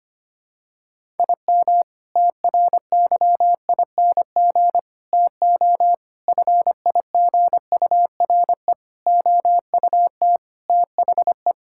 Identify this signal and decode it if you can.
Morse code, heard as a keyed tone, about 25 wpm, 700 Hz. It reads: IM TRYING TO FIGURE OUT THE
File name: practice / 25wpm / Sentences with 4-Word Phrases - 25wpm